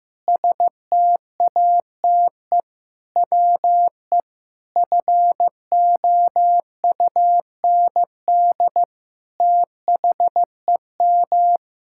Morse code, heard as a keyed tone, about 15 wpm, 700 Hz. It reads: STATE WE FOUND THEM
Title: Sentences from Top 1000 Words - Review Part 6 - 15wpm